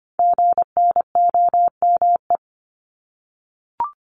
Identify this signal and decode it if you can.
Morse code, heard as a keyed tone, about 25 wpm, 700 Hz. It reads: GNOME
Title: Multi-letter Phonemes 25wpm